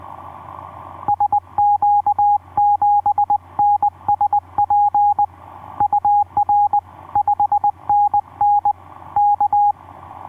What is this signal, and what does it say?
Morse code, heard as a keyed tone, about 20 wpm, 820 Hz. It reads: SQ7NSP UR 5NN K
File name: v5-dk1ce sq7nsp etc 12m cw 02jan2011 1320